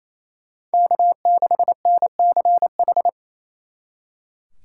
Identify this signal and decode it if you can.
Morse code, heard as a keyed tone, about 34 wpm, 700 Hz. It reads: K6NCH
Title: Speed Racing - USA Call Signs 28wpm